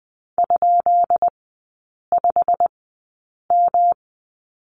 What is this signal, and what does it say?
Morse code, heard as a keyed tone, about 20 wpm, spaced as for 8 wpm, 700 Hz. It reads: ?5M
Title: 178 Lesson 29e - Slash - Three Characters 20wpm